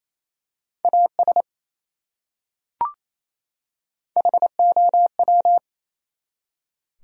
Morse code, heard as a keyed tone, about 28 wpm, 700 Hz. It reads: AS HOW
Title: Top 100 Words 28wpm